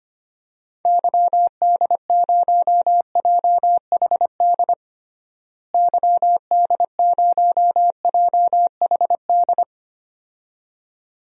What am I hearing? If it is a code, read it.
Morse code, heard as a keyed tone, about 25 wpm, 700 Hz. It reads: YD0JHD YD0JHD
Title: DXCC-100 Call Signs - 6 Characters - Repeated 2x - 25wpm